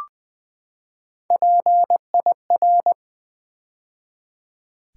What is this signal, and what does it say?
Morse code, heard as a keyed tone, about 20 wpm, 700 Hz. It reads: PIR
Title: Build A Word - 3 Letters At A Time - 6-Letter Words - 20wpm